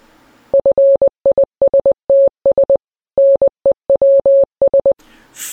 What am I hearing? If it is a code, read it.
Morse code, heard as a keyed tone, about 20 wpm, 550 Hz. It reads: FISTS NEWS